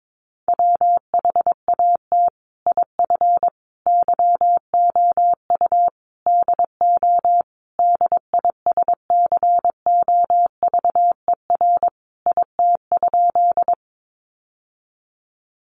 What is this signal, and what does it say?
Morse code, heard as a keyed tone, about 22 wpm, 700 Hz. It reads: WHAT IF YOU DO DISCOVER IT?
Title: Sentences from Top 1100 Words 22wpm